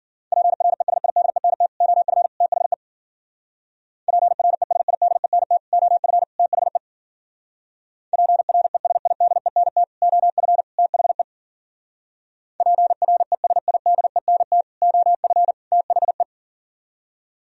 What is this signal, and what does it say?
Morse code, heard as a keyed tone, about 51 wpm, 700 Hz. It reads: PRESIDENT OF THE PRESIDENT OF THE PRESIDENT OF THE PRESIDENT OF THE